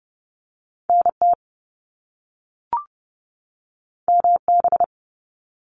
Morse code, heard as a keyed tone, about 30 wpm, 700 Hz. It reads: NT MB
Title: Canadian Provinces and Territories 30wpm